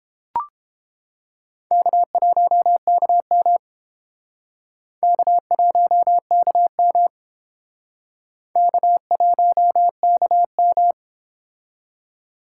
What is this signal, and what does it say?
Morse code, heard as a keyed tone, about 28 wpm, 700 Hz. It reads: K1KM K1KM K1KM